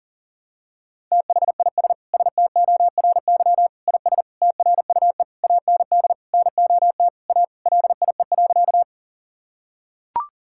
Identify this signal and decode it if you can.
Morse code, heard as a keyed tone, about 40 wpm, 700 Hz. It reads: THIS STORY IS TRUE AND NOT A LIE.
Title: Sentences from Top 1600 Words 40wpm